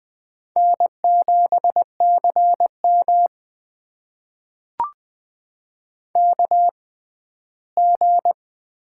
Morse code, heard as a keyed tone, about 20 wpm, spaced as for 14 wpm, 700 Hz. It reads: N7CM KG
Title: USA Call Signs - 6x Character Spacing - 20wpm